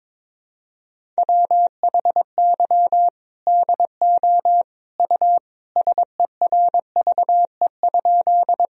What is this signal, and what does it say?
Morse code, heard as a keyed tone, about 21 wpm, 700 Hz. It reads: WHY DO U SERVE?
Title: Sentences from Top 400 Words and Abbreviations 22wpm